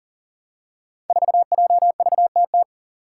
Morse code, heard as a keyed tone, about 40 wpm, 700 Hz. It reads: 4JVTT
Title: Five Challenging Letter-Number Combinations 40wpm